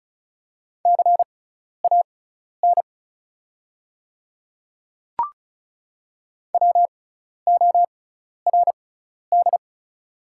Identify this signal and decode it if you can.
Morse code, heard as a keyed tone, about 36 wpm, spaced as for 12 wpm, 700 Hz. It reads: CAN WORD
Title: 6x Character Spacing - Top 100 Words - 35wpm